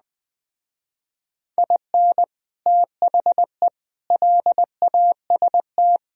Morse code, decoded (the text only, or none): IN THE LAST